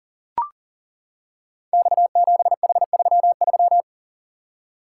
Morse code, heard as a keyed tone, about 40 wpm, 700 Hz. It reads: X7H33
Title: Code Groups - 5 Letter:Numbers - 40wpm